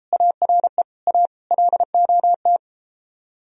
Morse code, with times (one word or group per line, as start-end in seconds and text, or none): 0.1-0.8 ARE
1.1-1.3 A
1.5-2.6 LOT